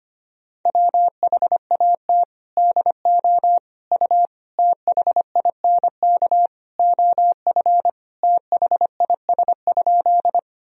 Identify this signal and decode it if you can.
Morse code, heard as a keyed tone, about 25 wpm, 700 Hz. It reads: WHAT DO U THINK OF THIS?
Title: Sentences from Top 200 Words and Abbreviations 25wpm